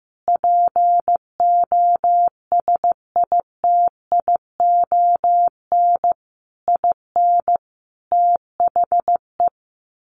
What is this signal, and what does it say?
Morse code, heard as a keyed tone, about 15 wpm, 700 Hz. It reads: POSITION IN THE
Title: Sentences from Top 1300 Words 15wpm